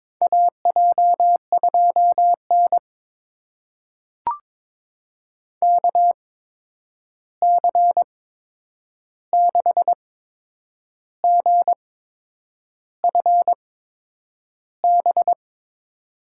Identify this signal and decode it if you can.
Morse code, heard as a keyed tone, about 22 wpm, spaced as for 8 wpm, 700 Hz. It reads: AJ2N KC6GFB
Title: USA Call Signs - 8x Character Spacing - 22wpm